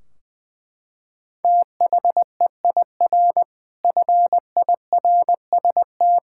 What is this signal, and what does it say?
Morse code, heard as a keyed tone, about 20 wpm, 700 Hz. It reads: THEIR FIRST
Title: Sentences from Top 1600 Words 20wpm